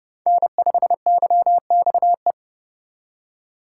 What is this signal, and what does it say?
Morse code, heard as a keyed tone, about 29 wpm, 700 Hz. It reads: N5YXE